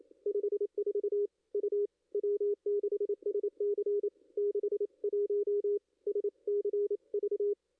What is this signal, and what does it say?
Morse code, heard as a keyed tone, about 27 wpm, 440 Hz. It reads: 54 U W6SC 61 SCV
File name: contests / ss-cw-k5tr-op-n5rz -2300